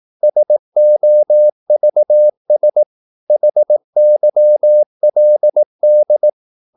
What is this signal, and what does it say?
Morse code, heard as a keyed tone, about 18 wpm, 600 Hz. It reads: SOVS HYLD
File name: Korte ord DK 18wpm